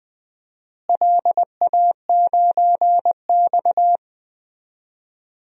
Morse code, heard as a keyed tone, about 20 wpm, 700 Hz. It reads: LA9X
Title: DXCC-100 Call Signs - 4 Characters - Repeated 2x - 20wpm